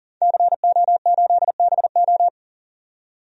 Morse code, heard as a keyed tone, about 37 wpm, 700 Hz. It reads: CO8BO
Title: DXCC-100 Call Signs - 5 Characters - 40wpm